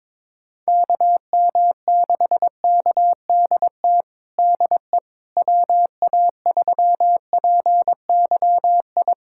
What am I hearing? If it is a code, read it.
Morse code, heard as a keyed tone, about 22 wpm, 700 Hz. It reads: KM6KDT DE WA3PYI
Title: USA Callsign DE Callsign - 6 Characters - 22wpm